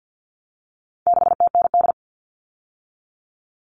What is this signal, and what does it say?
Morse code, heard as a keyed tone, about 50 wpm, 700 Hz. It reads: 6TND
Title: All Cut Numbers - 4-Digit 50wpm